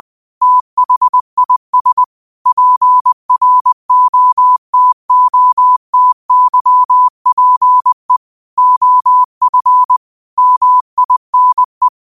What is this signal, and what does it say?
Morse code, heard as a keyed tone, about 20 wpm, 1000 Hz. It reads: THIS PROTOTYPE OF MINE
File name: qotdfiles / 20wpm